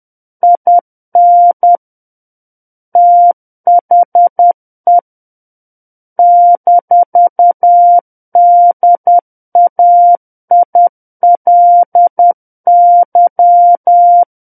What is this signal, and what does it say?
Morse code, heard as a keyed tone, about 10 wpm, 700 Hz. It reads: IN THE -DAILY
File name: Book One / 01-Chapter 1 - 10 WPM